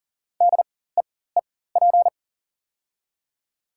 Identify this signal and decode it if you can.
Morse code, heard as a keyed tone, about 40 wpm, spaced as for 19 wpm, 700 Hz. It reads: DEEP
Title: 4x Character Spacing - Top 500 Words - 40wpm